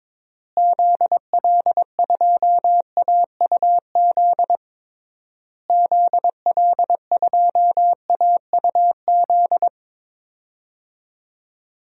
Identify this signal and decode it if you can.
Morse code, heard as a keyed tone, about 22 wpm, 700 Hz. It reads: ZL2AUZ ZL2AUZ
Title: DXCC-100 Call Signs - 6 Characters - Repeated 3x - 22wpm